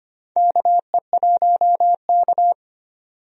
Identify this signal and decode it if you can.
Morse code, heard as a keyed tone, about 25 wpm, 700 Hz. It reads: KE1K
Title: USA Call Signs - 4 Characters - 25wpm